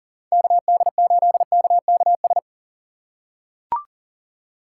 Morse code, heard as a keyed tone, about 40 wpm, 700 Hz. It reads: KD8KKS
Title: USA Call Signs - 6 Characters - 3x Character Spacing - 40wpm